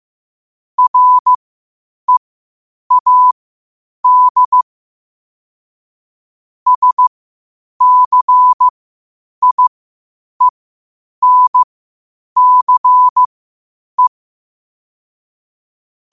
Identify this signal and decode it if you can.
Morse code, heard as a keyed tone, about 15 wpm, spaced as for 8 wpm, 1000 Hz. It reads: READ SCIENCE
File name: qotdfiles / 8wpm